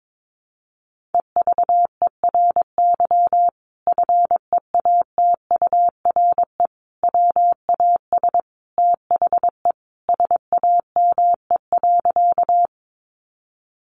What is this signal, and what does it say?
Morse code, heard as a keyed tone, about 22 wpm, 700 Hz. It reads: EVERY FEATURE WAS THE SAME.
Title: Sentences from Top 800 Words 22wpm